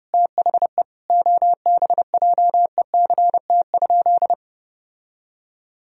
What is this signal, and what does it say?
Morse code, heard as a keyed tone, about 32 wpm, 700 Hz. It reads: THE OBJECT?
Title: Sentences from Top 1000 Words - Review Part 3 - 30wpm